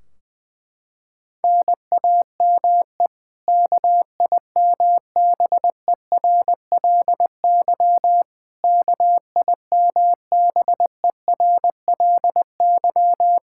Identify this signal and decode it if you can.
Morse code, heard as a keyed tone, about 21 wpm, 700 Hz. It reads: NAME KIMBERLY KIMBERLY
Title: Practice QSO - Exchange 1 - Beginning - 20wpm